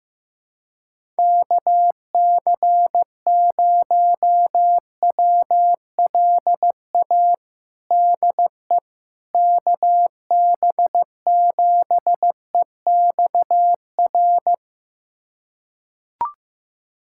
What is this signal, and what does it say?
Morse code, heard as a keyed tone, about 15 wpm, 700 Hz. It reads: KC0WLA DE KB7EXR
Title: USA Callsign DE Callsign 15wpm